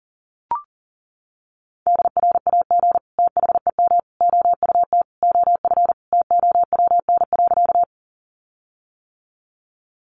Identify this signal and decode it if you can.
Morse code, heard as a keyed tone, about 40 wpm, 700 Hz. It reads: DRAG THEM OUT OF TOWN.
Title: Sentences from Top 1500 Words 40wpm